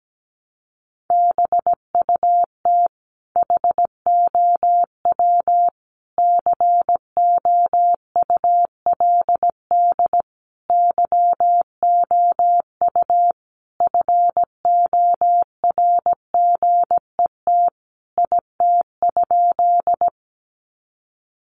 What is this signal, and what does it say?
Morse code, heard as a keyed tone, about 17 wpm, 700 Hz. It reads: BUT HOW COULD YOU FORGET IT?
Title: Sentences from Top 1200 Words 17wpm